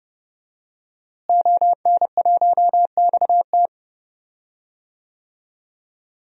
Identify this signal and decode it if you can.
Morse code, heard as a keyed tone, about 30 wpm, 700 Hz. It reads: ON1XT
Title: DXCC-100 Call Signs - 5 Characters - Repeated 2x - 30wpm